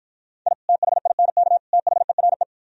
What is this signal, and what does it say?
Morse code, heard as a keyed tone, about 52 wpm, 700 Hz. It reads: I THINK THERE